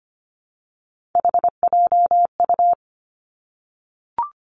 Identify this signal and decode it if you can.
Morse code, heard as a keyed tone, about 25 wpm, 700 Hz. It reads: HJU
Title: Three Challenging Letter-Number Combinations 25wpm